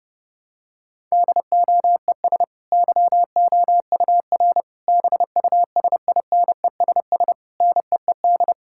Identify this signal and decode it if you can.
Morse code, heard as a keyed tone, about 28 wpm, 700 Hz. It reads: DOES YOUR BUSINESS NEED
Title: Sentences from Top 1000 Words - Review Part 6 - 30wpm